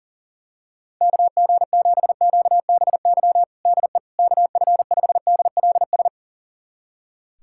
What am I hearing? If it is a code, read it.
Morse code, heard as a keyed tone, about 40 wpm, 700 Hz. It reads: KG7QBY DE KF5DLS
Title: USA Callsign DE Callsign - 6 Characters - 40wpm